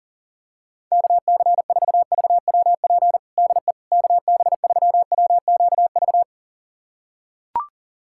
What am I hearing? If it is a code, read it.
Morse code, heard as a keyed tone, about 37 wpm, 700 Hz. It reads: KC4VWP DE KB3WQV
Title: USA Callsign DE Callsign 40wpm